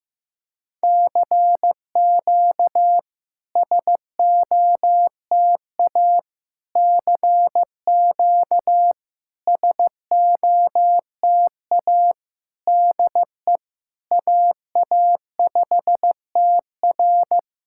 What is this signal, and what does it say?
Morse code, heard as a keyed tone, about 15 wpm, 690 Hz. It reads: CQ SOTA CQ SOTA DE AA5TR
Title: SOTA QSO Practice 15wpm